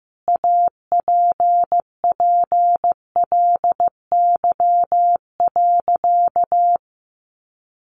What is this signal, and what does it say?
Morse code, heard as a keyed tone, about 15 wpm, 700 Hz. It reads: APPLY.
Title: Sentences from Top 1400 Words 15wpm